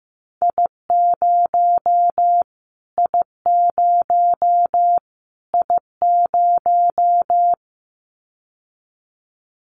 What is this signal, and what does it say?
Morse code, heard as a keyed tone, about 15 wpm, 700 Hz. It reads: I0 I0 I0
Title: Callsign Prefixes for Top 25 DXCC - Repeated 3x - 15wpm